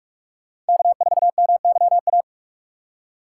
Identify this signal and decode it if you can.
Morse code, heard as a keyed tone, about 57 wpm, 700 Hz. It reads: K4MYA